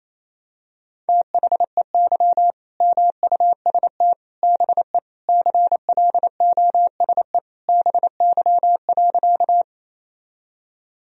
Morse code, heard as a keyed tone, about 28 wpm, 700 Hz. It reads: THEY MUST BE CLOSE BY.
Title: Sentences from Top 1000 Words - Review Part 1 - 28wpm